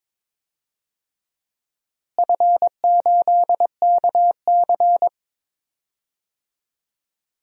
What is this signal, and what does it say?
Morse code, heard as a keyed tone, about 21 wpm, 700 Hz. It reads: F8KC
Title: DXCC-100 Call Signs - 4 Characters - Repeated 3x - 22wpm